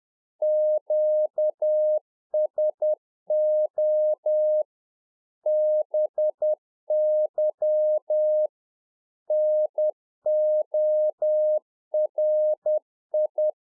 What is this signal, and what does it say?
Morse code, heard as a keyed tone, about 10 wpm, 610 Hz. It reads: QSO BY NORI
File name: 2011-04-mp3 / UNFOGETTABLE QSO 10WPM